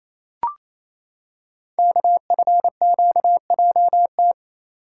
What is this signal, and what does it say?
Morse code, heard as a keyed tone, about 29 wpm, 690 Hz. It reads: KFQJT
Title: Code Groups - 5 Letters - 28wpm